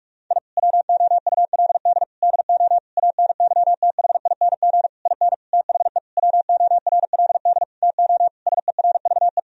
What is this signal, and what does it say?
Morse code, heard as a keyed tone, about 46 wpm, 700 Hz. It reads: I WOULD DO ANYTHING IN THE WORLD TO SERVE
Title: Sentences from Top 1000 Words - Review Part 2 - 45wpm